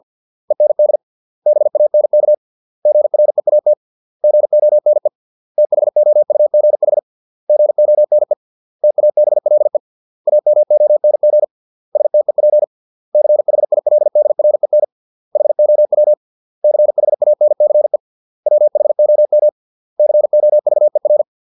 Morse code, heard as a keyed tone, about 50 wpm, 600 Hz. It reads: END BANK GREAT GONE THOUGH GONE TABLE AMONG STEP CHILDREN HOW CHANCE WHOM COVER